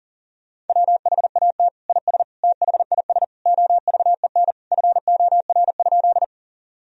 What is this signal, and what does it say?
Morse code, heard as a keyed tone, about 40 wpm, 700 Hz. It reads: WHAT IS THIS OVEN FOR?
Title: Sentences from Top 1000 Words - Review Part 4 - 40wpm